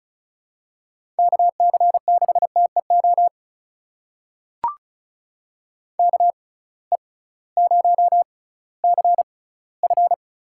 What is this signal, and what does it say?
Morse code, heard as a keyed tone, about 34 wpm, spaced as for 19 wpm, 700 Hz. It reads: KC6TEO KE0CF
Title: USA Call Signs - 6 Characters - 6x Character Spacing - 35wpm